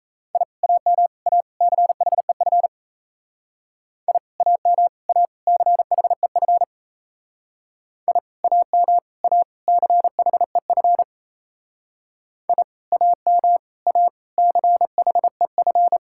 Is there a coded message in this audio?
Yes: Morse code from 0.4 to 16.0 s, heard as a keyed tone, about 36 wpm, 700 Hz. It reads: I AM A CHEF I AM A CHEF I AM A CHEF I AM A CHEF